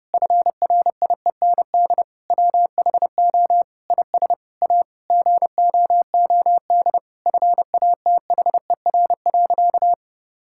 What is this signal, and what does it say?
Morse code, heard as a keyed tone, about 30 wpm, 700 Hz. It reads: FRIEND WHO IS A GOOD FATHER.
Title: Sentences from Top 1000 Words - Review Part 5 - 30wpm